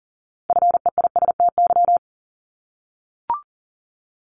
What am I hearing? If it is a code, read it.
Morse code, heard as a keyed tone, about 40 wpm, 700 Hz. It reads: FEISTY
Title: Multi-letter Phonemes Repeated 3x 40wpm